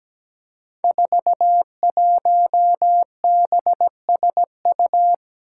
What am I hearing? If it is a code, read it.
Morse code, heard as a keyed tone, about 17 wpm, 700 Hz. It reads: 41BSU
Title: Five Challenging Letter-Number Combinations 17wpm